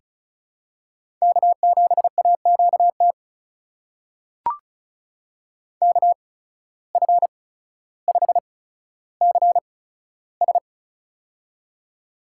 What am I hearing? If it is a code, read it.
Morse code, heard as a keyed tone, about 36 wpm, 700 Hz. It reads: K 7 A Q T K F 5 C S
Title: USA Call Signs - 8x Character Spacing - 35wpm